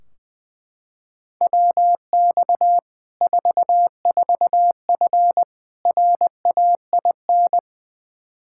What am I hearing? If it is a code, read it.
Morse code, heard as a keyed tone, about 20 wpm, 700 Hz. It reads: WX 44F RAIN
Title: Practice QSO - Exchange 2 - Info - 20wpm